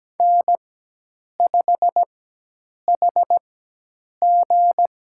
Morse code, heard as a keyed tone, about 17 wpm, spaced as for 8 wpm, 700 Hz. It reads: N5HG
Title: USA Call Signs - 4x Character Spacing - 17wpm